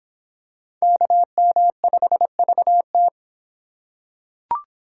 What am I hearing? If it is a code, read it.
Morse code, heard as a keyed tone, about 25 wpm, 700 Hz. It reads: KM5VT